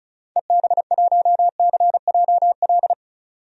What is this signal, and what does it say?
Morse code, heard as a keyed tone, about 36 wpm, 700 Hz. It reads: EB1CJL
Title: DXCC-100 Call Signs - 6 Characters - 35wpm